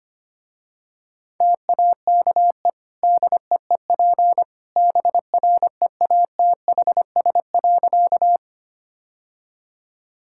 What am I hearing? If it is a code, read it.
Morse code, heard as a keyed tone, about 25 wpm, 700 Hz. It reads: TAKE DEEP BREATHS.
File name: Easy Sentences with Hints - 25wpm